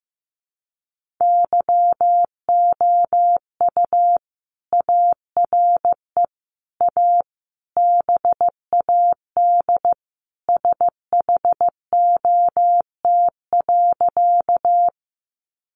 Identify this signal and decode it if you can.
Morse code, heard as a keyed tone, about 15 wpm, 700 Hz. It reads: YOU ARE A BAD SHOT.
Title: Sentences from Top 1100 Words 15wpm